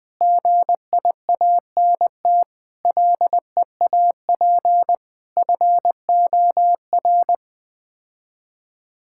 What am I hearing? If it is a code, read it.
Morse code, heard as a keyed tone, about 20 wpm, 700 Hz. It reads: GIANT LEAP FOR
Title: Sets of 9 Words - 20wpm